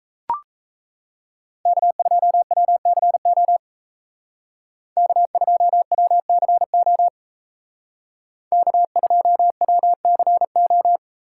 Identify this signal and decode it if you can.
Morse code, heard as a keyed tone, about 36 wpm, 700 Hz. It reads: K2WCO K2WCO K2WCO